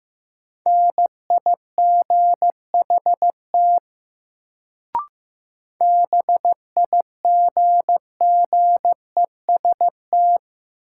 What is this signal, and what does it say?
Morse code, heard as a keyed tone, about 15 wpm, 700 Hz. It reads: NIGHT BIGGEST
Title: Sets of 4 Words - New Word Drill - 15wpm